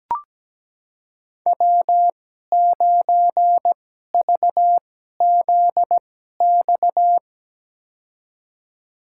Morse code, heard as a keyed tone, about 17 wpm, spaced as for 12 wpm, 700 Hz. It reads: W9VZX
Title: USA Call Signs - 5 Characters - 2x Character Spacing - 17wpm